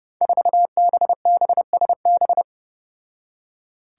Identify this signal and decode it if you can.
Morse code, heard as a keyed tone, about 32 wpm, 700 Hz. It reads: 4BBSB
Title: Five Challenging Letter-Number Combinations 30wpm